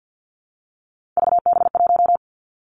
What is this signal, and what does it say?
Morse code, heard as a keyed tone, about 50 wpm, 700 Hz. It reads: VB1
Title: Three Challenging Letter-Number Combinations 50wpm